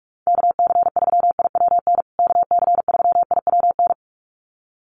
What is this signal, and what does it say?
Morse code, heard as a keyed tone, about 44 wpm, 700 Hz. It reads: KC3IWN KC3IWN
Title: USA Call Signs Repeated 2x 45wpm